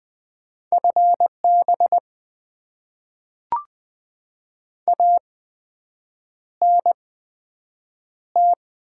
Morse code, heard as a keyed tone, about 20 wpm, spaced as for 7 wpm, 700 Hz. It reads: FB ANT
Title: Top 20 QSO Elements - 8x Character Spacing - 20wpm